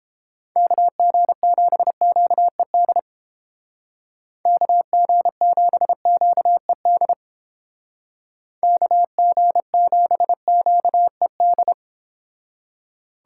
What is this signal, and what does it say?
Morse code, heard as a keyed tone, about 29 wpm, 700 Hz. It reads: KG7QED KG7QED KG7QED